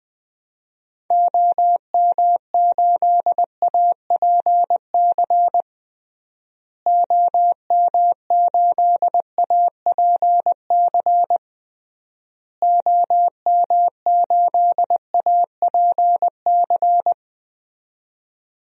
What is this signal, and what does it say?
Morse code, heard as a keyed tone, about 20 wpm, 700 Hz. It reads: OM8APC OM8APC OM8APC
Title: DXCC-100 Call Signs - 6 Characters - Repeated 3x - 20wpm